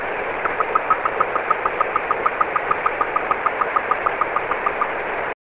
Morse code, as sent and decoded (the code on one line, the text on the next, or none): none